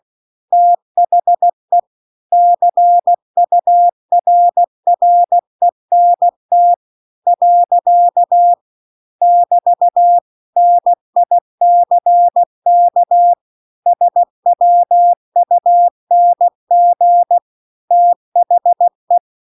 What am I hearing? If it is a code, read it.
Morse code, heard as a keyed tone, about 16 wpm, 700 Hz. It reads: THE CURRENT . <BT> NICK SWUNG THE
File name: CW-The-Big-Twoheaded-River-2-16WPM